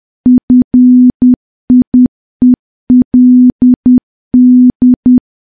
Morse code, heard as a keyed tone, about 10 wpm, 250 Hz. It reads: FIELD
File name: morse / morse-training-10WPM